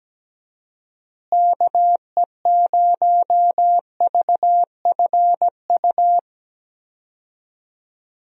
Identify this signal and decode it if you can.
Morse code, heard as a keyed tone, about 17 wpm, 700 Hz. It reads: KE0VFU
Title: USA Call Signs - 6 Characters - Repeated 2x - 17wpm